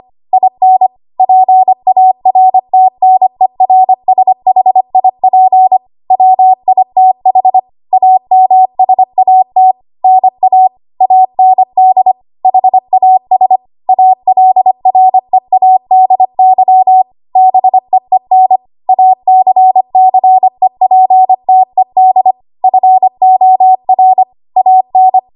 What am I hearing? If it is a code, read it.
Morse code, heard as a keyed tone, about 25 wpm, 750 Hz. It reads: IN PARTNERSHIP WITH AMSAT NA AND HAS ALREADY BEEN ACCEPTED FOR AN
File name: Archive / 25 WPM / 25WPM